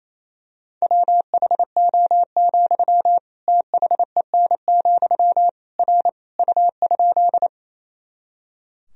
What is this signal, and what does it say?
Morse code, heard as a keyed tone, about 28 wpm, 700 Hz. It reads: WHO, THEN, R U?
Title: Sentences from Top 100 Words and Abbreviations 28wpm